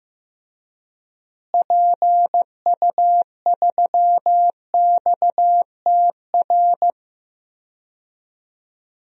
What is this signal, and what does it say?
Morse code, heard as a keyed tone, about 15 wpm, 700 Hz. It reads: PU3XTR
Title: DXCC-100 Call Signs - 6 Characters - Repeated 2x - 15wpm